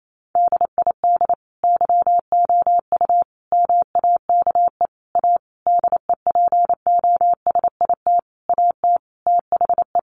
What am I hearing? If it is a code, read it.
Morse code, heard as a keyed tone, about 28 wpm, 700 Hz. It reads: DID YOU MAKE A DEPOSIT AT THE
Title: Sentences from Top 1300 Words 28wpm